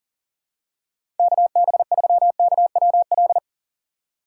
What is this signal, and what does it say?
Morse code, heard as a keyed tone, about 40 wpm, 700 Hz. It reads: KB3KWL
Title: USA Call Signs - 2x Character Spacing - 40wpm